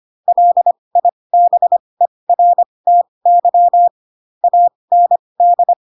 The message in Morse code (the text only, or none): LIBERTY AND